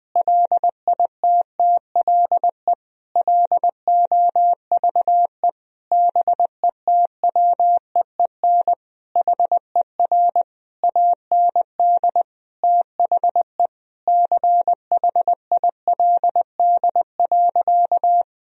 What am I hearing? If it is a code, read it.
Morse code, heard as a keyed tone, about 20 wpm, 700 Hz. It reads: LITTLE LOVE BETWEEN HER AND THE CHILD.
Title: Sentences from Top 1000 Words - Review Part 1 - 20wpm